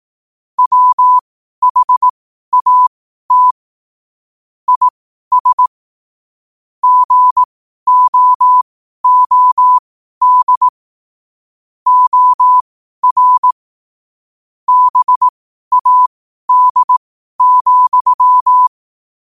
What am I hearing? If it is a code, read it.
Morse code, heard as a keyed tone, about 18 wpm, spaced as for 12 wpm, 1000 Hz. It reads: WHAT IS GOOD OR BAD,
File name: qotdfiles / 12wpm